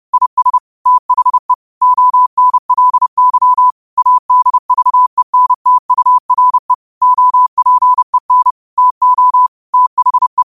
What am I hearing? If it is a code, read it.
Morse code, heard as a keyed tone, about 30 wpm, 1000 Hz. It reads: IS THE ONLY ADVENTURE OPEN TO THE
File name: qotdfiles / 30wpm